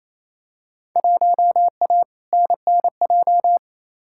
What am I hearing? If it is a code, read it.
Morse code, heard as a keyed tone, about 28 wpm, 700 Hz. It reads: 1A NNJ
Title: Field Day Exchanges Encore - 28wpm